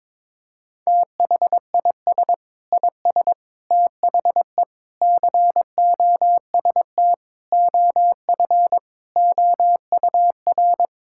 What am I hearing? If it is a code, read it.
Morse code, heard as a keyed tone, about 22 wpm, 700 Hz. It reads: THIS IS THE COST OF OUR
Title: Sentences from Top 1000 Words - Review Part 5 - 22wpm